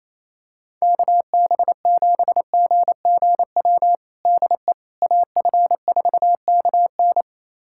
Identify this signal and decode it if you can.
Morse code, heard as a keyed tone, about 28 wpm, 700 Hz. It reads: KB7GGW DE AF4KN
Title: USA Callsign DE Callsign 28wpm